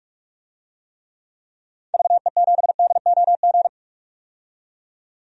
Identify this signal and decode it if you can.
Morse code, heard as a keyed tone, about 46 wpm, 700 Hz. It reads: VE7DOG
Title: DXCC-100 Call Signs - 6 Characters - 45wpm